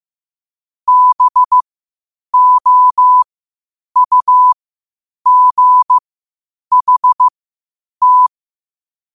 Morse code, heard as a keyed tone, about 15 wpm, spaced as for 8 wpm, 1000 Hz. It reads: BOUGHT
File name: qotdfiles / 8wpm